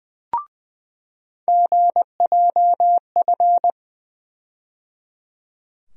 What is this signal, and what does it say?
Morse code, heard as a keyed tone, about 20 wpm, 700 Hz. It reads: GJF
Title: USA Call Sign Postfixes - 3 Letters 20wpm